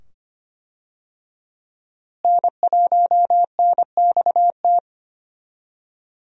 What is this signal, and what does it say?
Morse code, heard as a keyed tone, about 25 wpm, 700 Hz. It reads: N1NXT
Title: USA Call Signs - 5 Characters - Repeated 2x - 25wpm